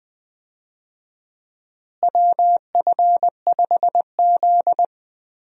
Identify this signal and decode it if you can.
Morse code, heard as a keyed tone, about 20 wpm, 700 Hz. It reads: WF5Z
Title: USA Call Signs - 4 Characters - Repeated 3x - 20wpm